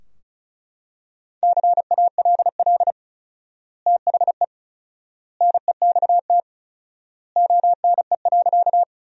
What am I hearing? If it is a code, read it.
Morse code, heard as a keyed tone, about 36 wpm, 700 Hz. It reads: CALL THE NEXT ONE.
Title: 4x Word-Spacing - Sentences from Top 300 Words - 35wpm